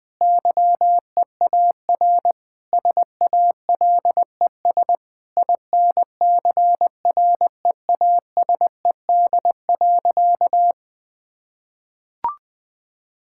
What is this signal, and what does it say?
Morse code, heard as a keyed tone, about 20 wpm, 700 Hz. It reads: YEAR SALES INCREASED.
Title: Sentences with 8-Word Phrases - Spoken Beginning - 20wpm